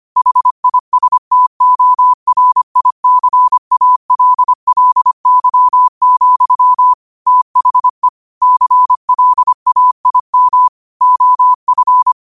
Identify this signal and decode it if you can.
Morse code, heard as a keyed tone, about 25 wpm, 1000 Hz. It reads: HISTORICALLY, THE CLAIM OF
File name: qotdfiles / 25wpm